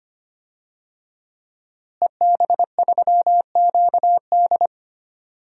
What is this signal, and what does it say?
Morse code, heard as a keyed tone, about 25 wpm, 700 Hz. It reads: EB3QD
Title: DXCC-100 Call Signs - 5 Characters - Repeated 2x - 25wpm